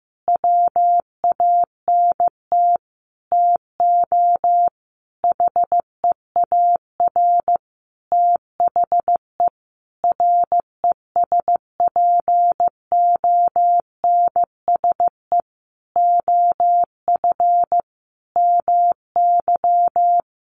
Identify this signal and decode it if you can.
Morse code, heard as a keyed tone, about 14 wpm, 700 Hz. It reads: WANT TO HEAR THE RESPONSE OF MY
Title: Sentences from Top 1400 Words 15wpm